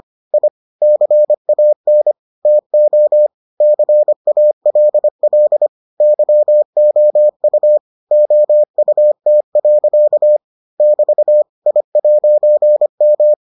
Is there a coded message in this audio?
Yes: Morse code from 0.3 to 13.3 s, heard as a keyed tone, about 25 wpm, 600 Hz. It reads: I CAN TO CALL YOU OUT. <BT> I'M